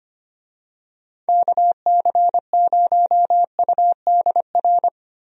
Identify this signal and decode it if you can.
Morse code, heard as a keyed tone, about 25 wpm, 700 Hz. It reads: KC0UDR
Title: USA Call Signs - 6 Characters - Repeated 2x - 25wpm